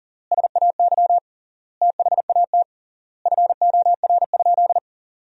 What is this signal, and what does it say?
Morse code, heard as a keyed tone, about 40 wpm, 700 Hz. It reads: SAY THAT FOR?
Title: 3x Word-Spacing - Sentences from Top 200 Words - 40wpm